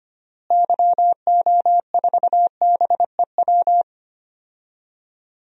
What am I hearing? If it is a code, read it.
Morse code, heard as a keyed tone, about 25 wpm, 700 Hz. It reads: YO4BEW
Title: DXCC-100 Call Signs - 6 Characters - Repeated 3x - 25wpm